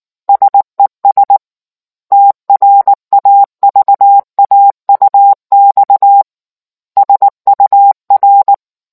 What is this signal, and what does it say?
Morse code, heard as a keyed tone, about 19 wpm, 800 Hz. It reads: SES TRAVAUX SUR